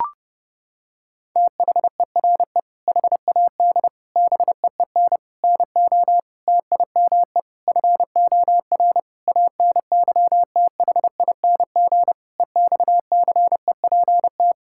Morse code, heard as a keyed tone, about 32 wpm, 700 Hz. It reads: THERE HAD BEEN NO TIME FOR ANYTHING EXCEPT
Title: Sentences from Top 1600 Words 30wpm